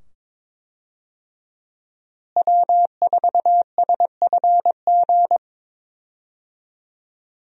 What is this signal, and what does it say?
Morse code, heard as a keyed tone, about 22 wpm, 700 Hz. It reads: W4SFG
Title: USA Call Signs - 5 Characters - Repeated 2x - 22wpm